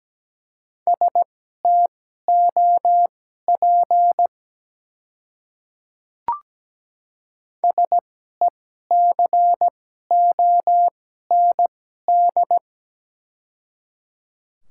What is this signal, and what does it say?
Morse code, heard as a keyed tone, about 17 wpm, spaced as for 12 wpm, 700 Hz. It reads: STOP SECOND
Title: 2x Character Spacing - Top 300 Words - 17wpm